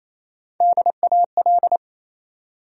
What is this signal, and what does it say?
Morse code, heard as a keyed tone, about 27 wpm, 700 Hz. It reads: DAL
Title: USA Call Sign Postfixes - 3 Letters 28wpm